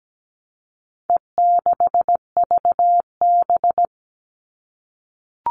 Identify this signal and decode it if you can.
Morse code, heard as a keyed tone, about 17 wpm, 700 Hz. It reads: E6VB
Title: All Cut Numbers - 4-Digit 17wpm